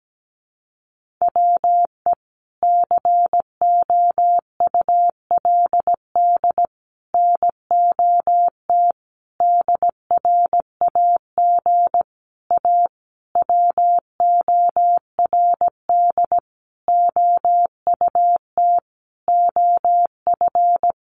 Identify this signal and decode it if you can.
Morse code, heard as a keyed tone, about 17 wpm, 700 Hz. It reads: WE COULD NOT DRAG A WORD OUT OF
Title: Sentences from Top 1500 Words 17wpm